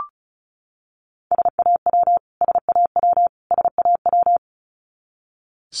Morse code, heard as a keyed tone, about 34 wpm, 700 Hz. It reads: SAW SAW SAW
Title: Top 300 Words Repeated 35wpm